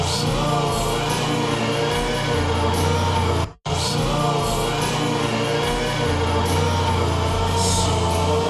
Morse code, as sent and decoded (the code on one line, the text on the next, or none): none